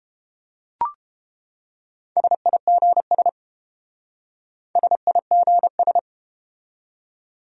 Morse code, heard as a keyed tone, about 30 wpm, 700 Hz. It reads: SIGS SIGS